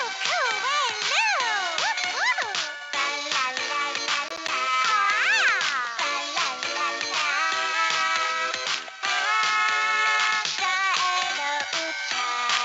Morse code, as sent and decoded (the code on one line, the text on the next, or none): none